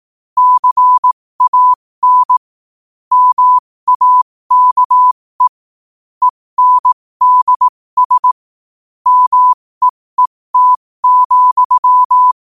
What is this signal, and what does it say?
Morse code, heard as a keyed tone, about 18 wpm, spaced as for 15 wpm, 1000 Hz. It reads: CAN MAKE ENDS MEET,
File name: qotdfiles / 15wpm